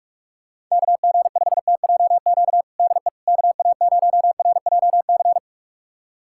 Morse code, heard as a keyed tone, about 46 wpm, 700 Hz. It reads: KG5TJQ DE KA0RJC
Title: USA Callsign DE Callsign - 6 Characters - 45wpm